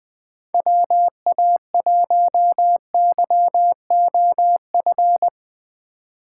Morse code, heard as a keyed tone, about 20 wpm, 700 Hz. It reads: WA1YOF
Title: USA Call Signs - 6 Characters - 4x Character Spacing - 20wpm